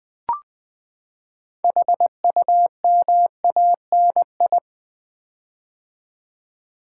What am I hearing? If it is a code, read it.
Morse code, heard as a keyed tone, about 20 wpm, 700 Hz. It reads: HUMANI
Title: Build A Word - 6 Letters At A Time - 12-Letter Words - 20wpm